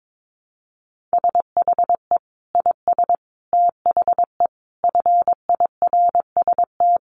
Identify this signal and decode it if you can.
Morse code, heard as a keyed tone, about 22 wpm, 700 Hz. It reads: SHE IS THE FIRST
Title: Sentences from Top 400 Words 22wpm